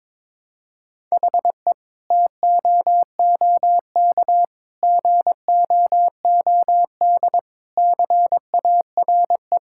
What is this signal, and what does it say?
Morse code, heard as a keyed tone, about 22 wpm, 700 Hz. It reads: HE TOOK GOOD CARE
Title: Sentences from Top 1000 Words - Review Part 5 - 22wpm